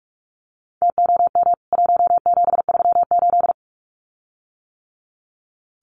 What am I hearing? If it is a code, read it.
Morse code, heard as a keyed tone, about 44 wpm, 700 Hz. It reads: TOM 1738
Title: CWT Exchange 45wpm